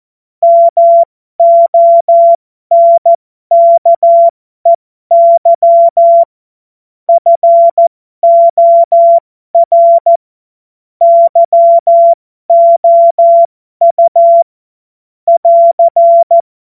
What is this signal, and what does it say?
Morse code, heard as a keyed tone, about 14 wpm, spaced as for 12 wpm, 680 Hz. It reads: MONKEY FOR YOU <AR>